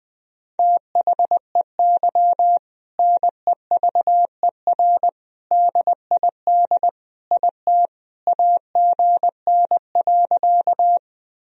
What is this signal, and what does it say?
Morse code, heard as a keyed tone, about 20 wpm, 700 Hz. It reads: THEY NEVER DID IT AGN.
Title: Sentences from Top 300 Words and Abbreviations 20wpm